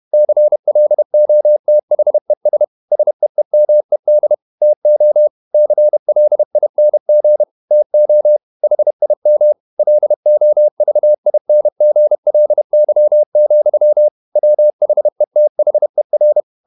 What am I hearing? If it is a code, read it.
Morse code, heard as a keyed tone, about 31 wpm, 600 Hz. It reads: CLOTHES SEEMED TO CLING TO HIM LOVINGLY, WHETHER
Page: Never 31wpm | CW med Gnister